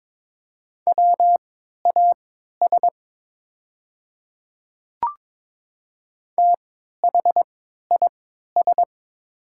Morse code, heard as a keyed tone, about 22 wpm, spaced as for 12 wpm, 690 Hz. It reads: WAS THIS
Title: 3x Character Spacing - Top 300 Words - 22wpm